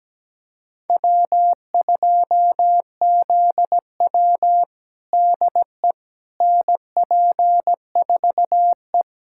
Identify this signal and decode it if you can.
Morse code, heard as a keyed tone, about 17 wpm, 700 Hz. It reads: W2ZW DE NP4E
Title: USA Callsign DE Callsign - 4 Characters - 17wpm